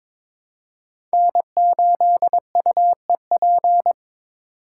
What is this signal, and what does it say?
Morse code, heard as a keyed tone, about 23 wpm, 700 Hz. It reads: N8UEP
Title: Code Groups - 5 Letter:Numbers - 22wpm